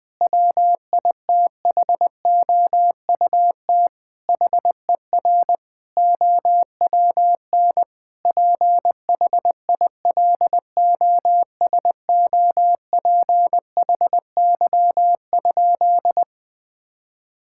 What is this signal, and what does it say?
Morse code, heard as a keyed tone, about 21 wpm, 700 Hz. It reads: WITHOUT HER OWN PHILOSOPHY?
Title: Sentences from Top 1000 Words - Review Part 6 - 20wpm